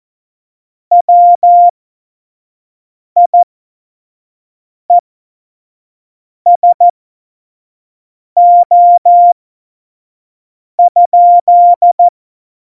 Morse code, heard as a keyed tone, about 14 wpm, spaced as for 5 wpm, 700 Hz. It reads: WIESO?
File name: Morsekurs / Witze / Witz 04 August 2020, 25 F70